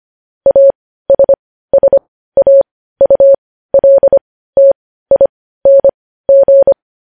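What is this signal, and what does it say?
Morse code, heard as a keyed tone, about 25 wpm, spaced as for 15 wpm, 550 Hz. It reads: ASSAULTING
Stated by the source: Morse code news headlines sent at 15, preceded by vvv.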